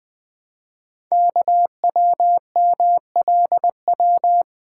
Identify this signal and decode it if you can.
Morse code, heard as a keyed tone, about 20 wpm, 700 Hz. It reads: KWMLW
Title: Code Groups - 5 Letters - 20wpm